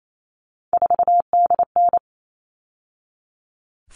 Morse code, heard as a keyed tone, about 28 wpm, 700 Hz. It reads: 4DN
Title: All Cut Numbers - 3-Digit 28wpm